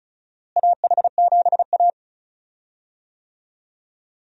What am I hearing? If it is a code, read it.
Morse code, heard as a keyed tone, about 34 wpm, 700 Hz. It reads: AH7A
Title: USA Call Signs - 4 Characters - Repeated 2x - 35wpm